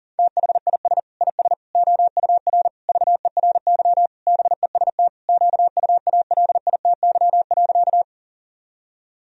Morse code, heard as a keyed tone, about 37 wpm, 700 Hz. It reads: THIS IS OUR VERY BEST QUALITY.
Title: Sentences from Top 1000 Words - Review Part 4 - 40wpm